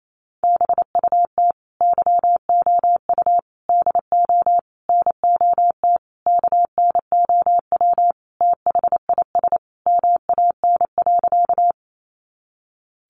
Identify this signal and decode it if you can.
Morse code, heard as a keyed tone, about 28 wpm, 700 Hz. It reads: BUT YOU DO NOT KNOW THIS MAN.
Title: Sentences from Top 600 Words 28wpm